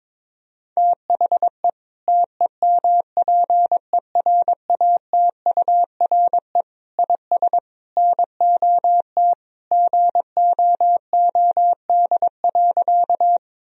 Morse code, heard as a keyed tone, about 21 wpm, 700 Hz. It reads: THE TEMPERATURE IS NOT GOOD.
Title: Sentences from Top 1000 Words - Review Part 4 - 22wpm